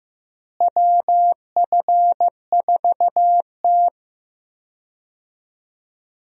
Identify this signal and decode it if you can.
Morse code, heard as a keyed tone, about 16 wpm, 700 Hz. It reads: WF4T
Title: USA Call Signs - 4 Characters - Repeated 3x - 15wpm